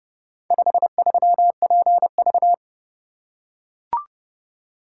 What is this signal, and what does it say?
Morse code, heard as a keyed tone, about 30 wpm, 700 Hz. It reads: 53PV